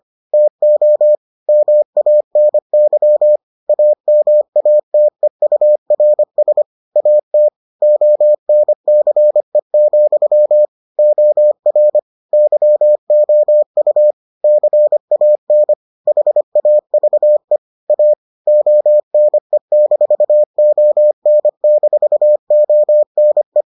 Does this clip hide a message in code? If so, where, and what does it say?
Yes: Morse code from 0.3 to 23.6 s, heard as a keyed tone, about 25 wpm, 600 Hz. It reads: TO MANY AMATEURS AT ONCE, OR YOU CAN HAVE A ONE-ON-ONE